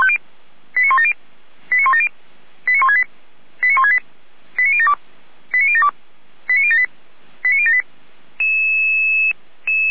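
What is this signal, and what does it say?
Five-tone selective call (ZVEI), heard as quick runs of five tones, tones of about 70 ms. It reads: 89170 89170 89168 89168 89072 89072 89078 89078
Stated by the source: Received by SDR; ZVEI five-tone call.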